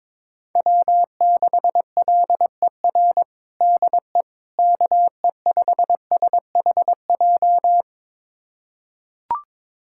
Morse code, heard as a keyed tone, about 22 wpm, 700 Hz. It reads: W6LER DE KE5SHJ
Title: USA Callsign DE Callsign 22wpm